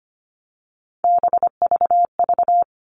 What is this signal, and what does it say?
Morse code, heard as a keyed tone, about 25 wpm, 700 Hz. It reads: BVV
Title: All Cut Numbers - 3-Digit 25wpm